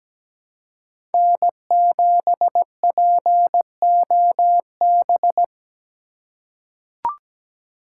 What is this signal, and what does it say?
Morse code, heard as a keyed tone, about 17 wpm, 700 Hz. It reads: N7POB
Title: USA Call Signs - 5 Characters - 3x Character Spacing - 17wpm